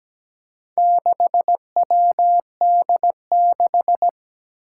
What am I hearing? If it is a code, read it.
Morse code, heard as a keyed tone, about 17 wpm, 700 Hz. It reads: 6WD6
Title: Four Challenging Letter-Number Combinations 17wpm